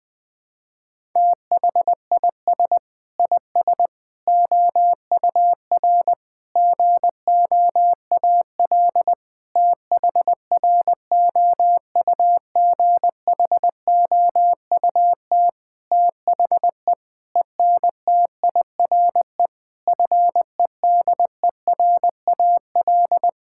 Morse code, heard as a keyed tone, about 21 wpm, 700 Hz. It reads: THIS IS OUR GOAL THROUGHOUT THE ENTIRE FEDERAL
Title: Sentences from Top 1000 Words - Review Part 5 - 20wpm